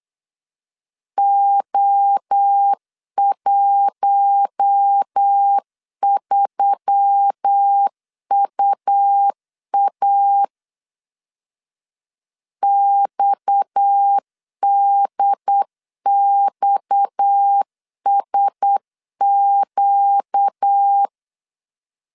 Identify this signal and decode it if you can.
Morse code, heard as a keyed tone, about 8 wpm, 800 Hz. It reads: O13UA XDXSQ
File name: LEZIONI / 6 PKX0 / 6 PKX 8wpm 3041